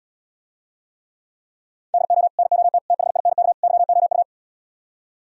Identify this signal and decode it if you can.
Morse code, heard as a keyed tone, about 77 wpm, 700 Hz. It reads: DO NOT THINK YOU